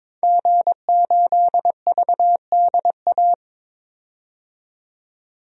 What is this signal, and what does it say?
Morse code, heard as a keyed tone, about 22 wpm, 700 Hz. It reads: G8VDA
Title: DXCC-100 Call Signs - 5 Characters - 22wpm